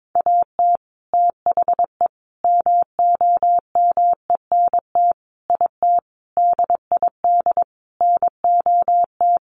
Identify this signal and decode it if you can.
Morse code, heard as a keyed tone, about 22 wpm, 700 Hz. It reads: AT THE MOMENT IT DID NOT
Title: Sentences from Top 1200 Words 22wpm